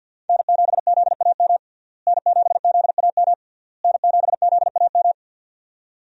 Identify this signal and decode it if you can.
Morse code, heard as a keyed tone, about 51 wpm, 700 Hz. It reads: N7ZAM N7ZAM N7ZAM
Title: USA Call Signs - 5 Characters - Repeated 3x - 50wpm